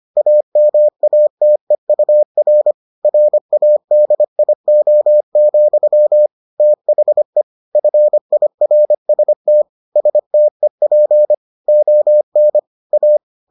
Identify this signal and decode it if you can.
Morse code, heard as a keyed tone, about 25 wpm, 600 Hz. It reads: AMATEUR RADIO, THE FIRST STEP ON A